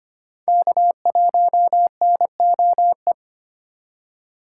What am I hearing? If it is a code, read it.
Morse code, heard as a keyed tone, about 25 wpm, 700 Hz. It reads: K1NOE
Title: USA Call Signs - 2x Character Spacing - 25wpm